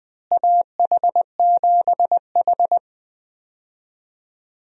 Morse code, heard as a keyed tone, about 20 wpm, 700 Hz. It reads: AH7H
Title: USA Call Signs - 4 Characters - Repeated 3x - 20wpm